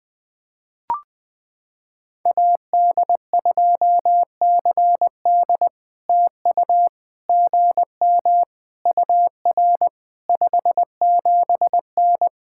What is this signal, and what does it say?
Morse code, heard as a keyed tone, about 20 wpm, 700 Hz. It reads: AD2CD TU GM UR 57N
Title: POTA QSO Practice - 20wpm